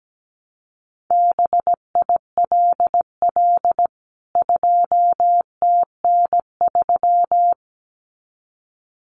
Morse code, heard as a keyed tone, about 17 wpm, 700 Hz. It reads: BILL 2TN3
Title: CWT Exchange 17wpm